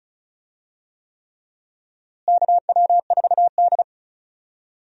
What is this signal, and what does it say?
Morse code, heard as a keyed tone, about 36 wpm, 700 Hz. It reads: KW4D
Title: USA Call Signs - 4 Characters - Repeated 3x - 35wpm